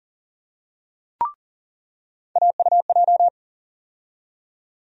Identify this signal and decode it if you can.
Morse code, heard as a keyed tone, about 39 wpm, 700 Hz. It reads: AUJ
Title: USA Call Sign Postfixes - 3 Letters 40wpm